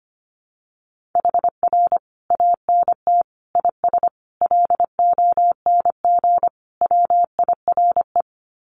Morse code, heard as a keyed tone, about 25 wpm, 700 Hz. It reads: HR ANT IS LONG WIRE
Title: Common QSO Phrases 25wpm